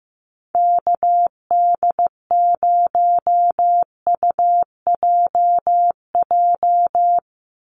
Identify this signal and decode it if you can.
Morse code, heard as a keyed tone, about 15 wpm, 700 Hz. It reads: KD0UJJ
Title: USA Call Signs 15wpm